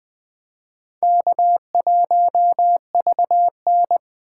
Morse code, heard as a keyed tone, about 21 wpm, 700 Hz. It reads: K1VN
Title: USA Call Signs - 3x Character Spacing - 20wpm